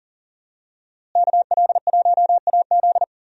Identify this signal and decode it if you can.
Morse code, heard as a keyed tone, about 37 wpm, 700 Hz. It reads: KL1AZ
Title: USA Call Signs - 5 Characters - Repeated 3x - 40wpm